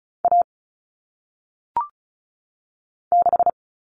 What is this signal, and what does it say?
Morse code, heard as a keyed tone, about 35 wpm, 700 Hz. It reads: A 6
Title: All Cut Numbers - 1-Digit 35wpm